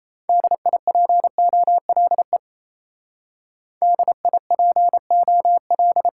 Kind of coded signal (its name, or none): Morse code